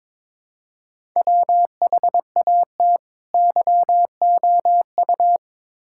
Morse code, heard as a keyed tone, about 22 wpm, 700 Hz. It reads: WHAT YOU
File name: Build A Phrase - 2-Word Phrases - 22wpm